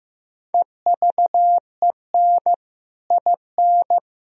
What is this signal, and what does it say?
Morse code, heard as a keyed tone, about 15 wpm, 700 Hz. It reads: EVEN IN
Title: Sentences from Top 1000 Words - Review Part 1 - 15wpm